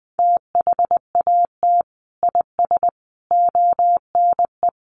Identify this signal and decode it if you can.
Morse code, heard as a keyed tone, about 20 wpm, 700 Hz. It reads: THAT IS ONE
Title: Sentences from Top 1200 Words 20wpm